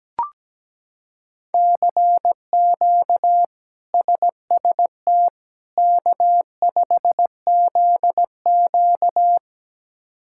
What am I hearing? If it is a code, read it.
Morse code, heard as a keyed tone, about 17 wpm, 700 Hz. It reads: CQ SST K5ZQ
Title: K1USN SST QSO Practice - 17wpm